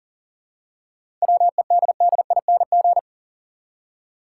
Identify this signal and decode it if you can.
Morse code, heard as a keyed tone, about 40 wpm, 700 Hz. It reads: WEDDING
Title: Seven Letter Words - 40wpm